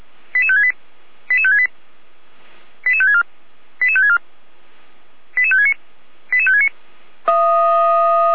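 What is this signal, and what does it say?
Five-tone selective call (ZVEI), heard as quick runs of five tones, tones of about 80 ms. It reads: 88568 88568 88564 88564 88569 88569
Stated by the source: ZVEI five-tone selective call 88108, received by SDR, from the feuerwehr-orlamuende.org fire service audio server.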